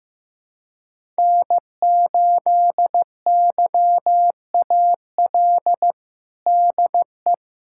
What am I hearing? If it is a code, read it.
Morse code, heard as a keyed tone, about 15 wpm, 700 Hz. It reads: N8YAL DE
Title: USA Callsign DE Callsign - 5 Characters - 15wpm